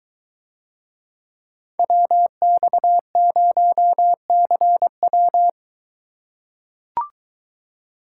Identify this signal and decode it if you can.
Morse code, heard as a keyed tone, about 23 wpm, 700 Hz. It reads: WX0CW